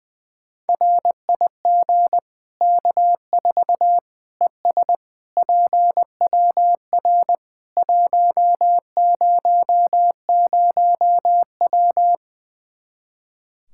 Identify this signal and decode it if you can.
Morse code, heard as a keyed tone, about 20 wpm, 700 Hz. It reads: RIG K4 ES PWR 100W
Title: Practice QSO - Exchange 2 - Info - 20wpm